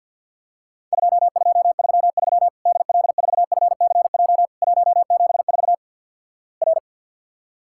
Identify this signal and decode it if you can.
Morse code, heard as a keyed tone, about 50 wpm, 700 Hz. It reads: 2233 DL4FCJ 174 R
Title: Worked All Europe DX Contest - QTC Exchange - 50wpm